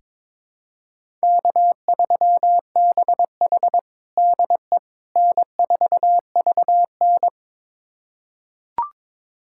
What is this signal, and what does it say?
Morse code, heard as a keyed tone, about 22 wpm, 700 Hz. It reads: K3BH DE N4VN
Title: USA Callsign DE Callsign - 4 Characters - 22wpm